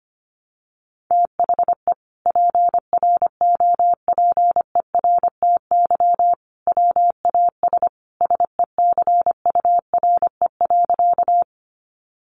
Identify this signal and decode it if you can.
Morse code, heard as a keyed tone, about 25 wpm, 700 Hz. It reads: THE PROPERTY WAS SECURE.
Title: Sentences from Top 1200 Words 25wpm